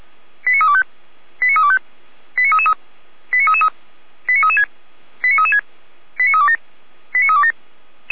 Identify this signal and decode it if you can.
Five-tone selective call (ZVEI), heard as quick runs of five tones, tones of about 70 ms. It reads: 89325 89325 89333 89333 89336 89336 89327 89327